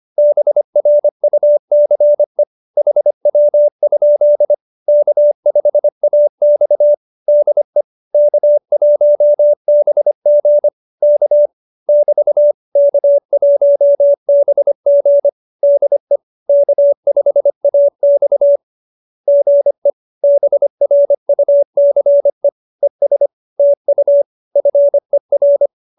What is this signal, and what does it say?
Morse code, heard as a keyed tone, about 25 wpm, 590 Hz. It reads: BRUCE HW? K5AX DE K1BG K <BT> K1BG DE K5AX GE BRUCE ES TU FER